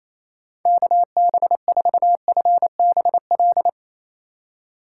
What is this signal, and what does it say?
Morse code, heard as a keyed tone, about 28 wpm, 700 Hz. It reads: KB4FBL
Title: USA Call Signs - 6 Characters - 4x Character Spacing - 28wpm